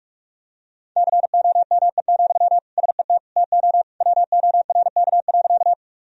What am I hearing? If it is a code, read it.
Morse code, heard as a keyed tone, about 46 wpm, 700 Hz. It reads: COME, SET TO WORK.
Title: Sentences from Top 1000 Words - Review Part 1 - 45wpm